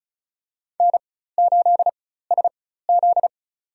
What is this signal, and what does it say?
Morse code, heard as a keyed tone, about 36 wpm, spaced as for 16 wpm, 700 Hz. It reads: N8SZ
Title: USA Call Signs - 4 Characters - 4x Character Spacing - 35wpm